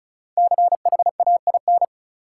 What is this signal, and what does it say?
Morse code, heard as a keyed tone, about 36 wpm, 700 Hz. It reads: CHAIN
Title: Five Letter Words - 35wpm